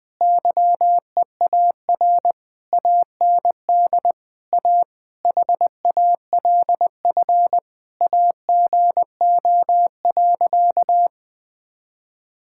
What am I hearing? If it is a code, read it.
Morse code, heard as a keyed tone, about 20 wpm, 700 Hz. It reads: YEAR AND A HALF AGO.
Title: Sentences with 6-Word Phrases - Spoken Beginning - 20wpm